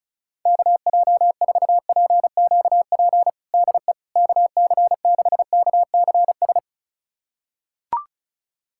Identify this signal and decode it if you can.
Morse code, heard as a keyed tone, about 34 wpm, 700 Hz. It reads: KJ4PQP DE KC6KCS
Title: USA Callsign DE Callsign - 6 Characters - 35wpm